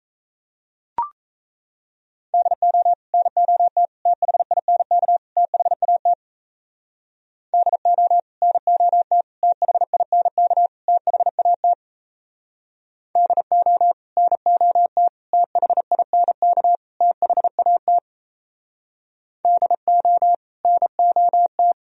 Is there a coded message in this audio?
Yes: Morse code from 2.3 to 21.7 s, heard as a keyed tone, about 37 wpm, 700 Hz. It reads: DO NOT THINK THAT DO NOT THINK THAT DO NOT THINK THAT DO NOT